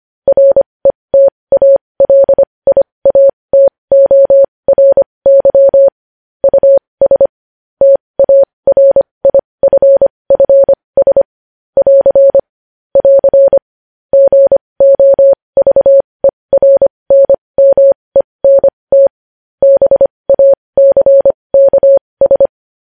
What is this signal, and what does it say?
Morse code, heard as a keyed tone, about 25 wpm, spaced as for 20 wpm, 550 Hz. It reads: RETALIATORY US TARIFFS <AR> <AR> GOVERNMENT BACKS
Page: News Headlines in Morse Code at 20 WPM